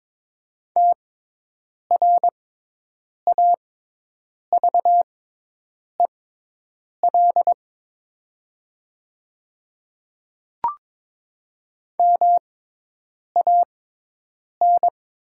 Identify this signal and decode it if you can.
Morse code, heard as a keyed tone, about 22 wpm, spaced as for 8 wpm, 700 Hz. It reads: TRAVEL MAN
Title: 6x Character Spacing - Top 500 Words - 22wpm